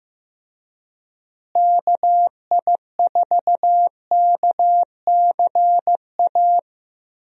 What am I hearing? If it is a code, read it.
Morse code, heard as a keyed tone, about 15 wpm, 700 Hz. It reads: KI4KCA
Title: USA Call Signs - 6 Characters - Repeated 3x - 15wpm